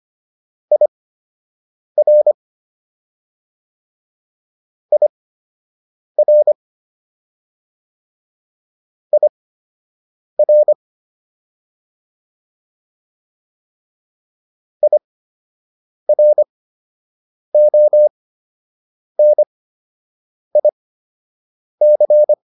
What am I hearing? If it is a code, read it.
Morse code, heard as a keyed tone, about 25 wpm, spaced as for 7 wpm, 600 Hz. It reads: IR IR IR IRONIC
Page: fundamental-practice-files